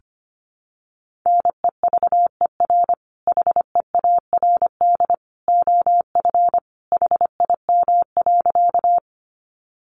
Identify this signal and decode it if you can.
Morse code, heard as a keyed tone, about 25 wpm, 700 Hz. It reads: NEVER HEARD OF HIM.
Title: Sentences from Top 400 Words 25wpm